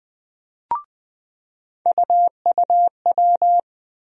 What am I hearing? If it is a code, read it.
Morse code, heard as a keyed tone, about 20 wpm, 700 Hz. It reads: UUW
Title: USA Call Sign Postfixes - 3 Letters 20wpm